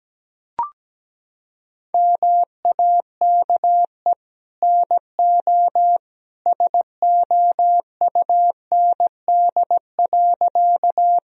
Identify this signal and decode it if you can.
Morse code, heard as a keyed tone, about 17 wpm, 700 Hz. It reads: MAKE NO SOUND.
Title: Sentences from Top 1000 Words - Review Part 1 - 17wpm